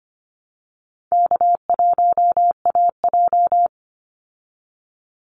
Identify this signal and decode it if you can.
Morse code, heard as a keyed tone, about 25 wpm, 700 Hz. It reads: K1AJ
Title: Call Signs Repeated 2x - 25wpm